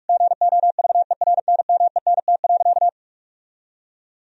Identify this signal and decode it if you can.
Morse code, heard as a keyed tone, about 46 wpm, 700 Hz. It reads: GOVERNMENT.
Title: Sentences from Top 1000 Words - Review Part 3 - 45wpm